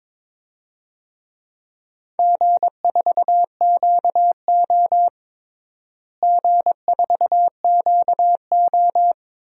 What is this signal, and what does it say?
Morse code, heard as a keyed tone, about 22 wpm, 700 Hz. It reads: G4QO G4QO
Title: DXCC-100 Call Signs - 4 Characters - Repeated 3x - 22wpm